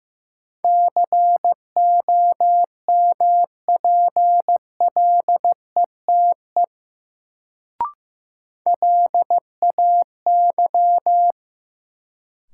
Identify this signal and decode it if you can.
Morse code, heard as a keyed tone, about 15 wpm, 700 Hz. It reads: COMPLETE LAY
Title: Sentences from Top 400 Words - New Word Drill - 15wpm